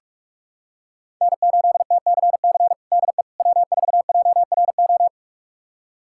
Morse code, heard as a keyed tone, about 46 wpm, 700 Hz. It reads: N8TCC DE W4JRO
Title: USA Callsign DE Callsign - 5 Characters - 45wpm